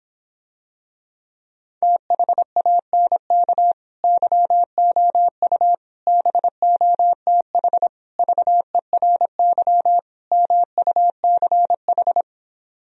Morse code, heard as a keyed tone, about 26 wpm, 700 Hz. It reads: THANK YOU BOTH VERY MUCH